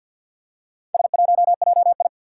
Morse code, heard as a keyed tone, about 47 wpm, 700 Hz. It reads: S1JI
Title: Four Challenging Letter-Number Combinations 50wpm